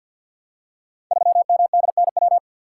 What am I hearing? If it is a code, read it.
Morse code, heard as a keyed tone, about 48 wpm, 700 Hz. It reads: 3MDNW
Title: Code Groups - 5 Letter:Numbers - 50wpm